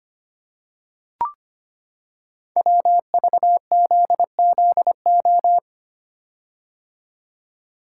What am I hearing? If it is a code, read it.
Morse code, heard as a keyed tone, about 25 wpm, 700 Hz. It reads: WVZZO
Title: Code Groups - 5 Letters - 25wpm